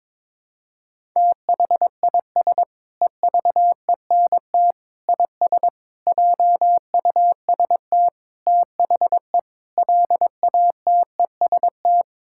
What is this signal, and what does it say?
Morse code, heard as a keyed tone, about 22 wpm, 700 Hz. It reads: THIS EVENT IS JUST THE LATEST
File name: practice / 22wpm / Sentences with 8-Word Phrases - 22wpm